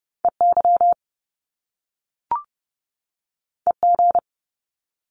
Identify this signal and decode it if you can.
Morse code, heard as a keyed tone, about 30 wpm, 700 Hz. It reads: EY EG
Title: Top Two Letter Combinations in Words 30wpm